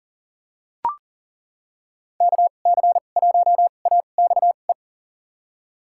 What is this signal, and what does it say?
Morse code, heard as a keyed tone, about 40 wpm, spaced as for 29 wpm, 700 Hz. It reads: KC1AXE
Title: USA Call Signs - 2x Character Spacing - 40wpm